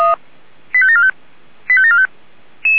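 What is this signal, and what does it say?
Five-tone selective call (ZVEI), heard as quick runs of five tones, tones of about 70 ms. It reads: 85745 85745
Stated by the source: SDR reception of ZVEI five-tone call 85726.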